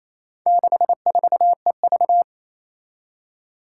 Morse code, heard as a keyed tone, about 28 wpm, 700 Hz. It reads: 64EV
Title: Four Challenging Letter-Number Combinations 28wpm